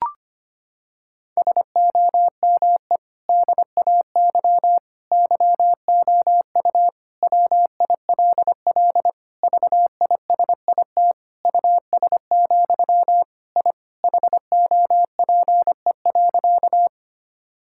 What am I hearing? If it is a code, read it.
Morse code, heard as a keyed tone, about 25 wpm, 700 Hz. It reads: SOME DAY YOU WILL VISIT US, I HOPE.
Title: Sentences from Top 1000 Words - Review Part 6 - 25wpm